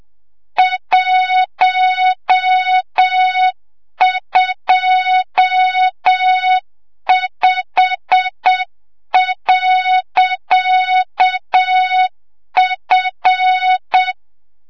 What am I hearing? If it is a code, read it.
Morse code, heard as a keyed tone, about 7 wpm, 3700 Hz, with noise. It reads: 125.F
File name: Morze-HA2MN / hang / 25 GYAKORLAT